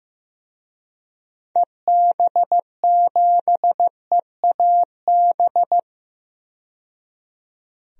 Morse code, heard as a keyed tone, about 15 wpm, 700 Hz. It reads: EB7EAB
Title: DXCC-100 Call Signs - 6 Characters - 15wpm